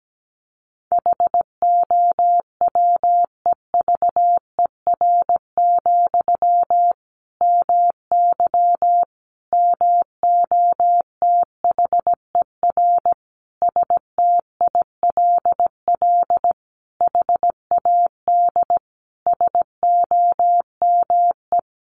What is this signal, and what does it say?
Morse code, heard as a keyed tone, about 17 wpm, 700 Hz. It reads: HOWEVER, MY MOTHER STILL HAD SOME
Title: Sentences from Top 1000 Words 17wpm